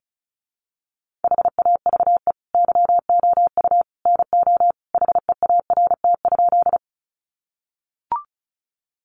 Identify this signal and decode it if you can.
Morse code, heard as a keyed tone, about 34 wpm, 700 Hz. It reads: HAVE YOU NO HEART?
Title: Sentences from Top 1200 Words 35wpm